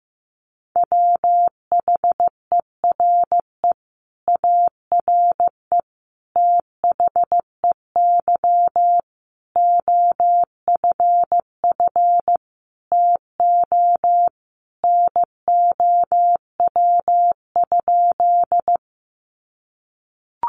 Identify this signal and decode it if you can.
Morse code, heard as a keyed tone, about 15 wpm, 700 Hz. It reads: WHERE ARE THEY OFF TO NOW?
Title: Sentences from Top 200 Words 15wpm